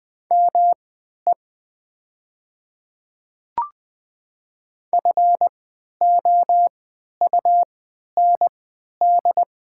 Morse code, heard as a keyed tone, about 20 wpm, spaced as for 11 wpm, 700 Hz. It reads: ME FOUND
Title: 3x Character Spacing - Top 300 Words - 20wpm